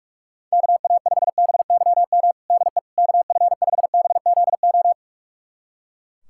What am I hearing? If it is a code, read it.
Morse code, heard as a keyed tone, about 42 wpm, 700 Hz. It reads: KA5BYM DE KF5BZO
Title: USA Callsign DE Callsign 45wpm